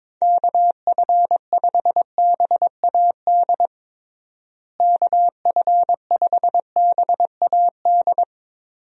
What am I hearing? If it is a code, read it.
Morse code, heard as a keyed tone, about 21 wpm, 700 Hz. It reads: KF5BAD KF5BAD
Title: USA Call Signs - 6 Characters - Repeated 2x - 22wpm